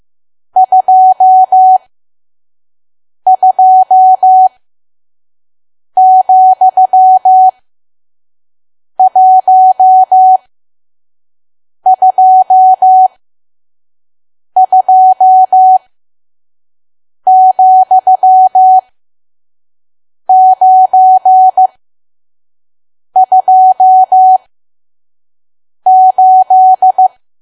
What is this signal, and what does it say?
Morse code, heard as a keyed tone, about 15 wpm, spaced as for 5 wpm, 750 Hz. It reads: 22,122,928
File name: Archive / 5 WPM files / PM